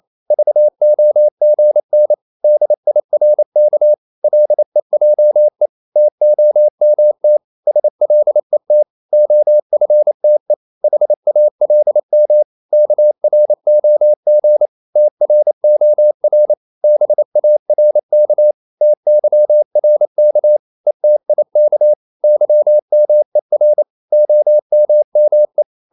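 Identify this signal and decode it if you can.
Morse code, heard as a keyed tone, about 28 wpm, 600 Hz. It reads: VOGN DIRK LEJE TOMT SLET OFTE HALM KROG TROR BARK TYRK ETIK YMER OMME
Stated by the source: Danish short words sent in Morse at 28 wpm, from CW med Gnister.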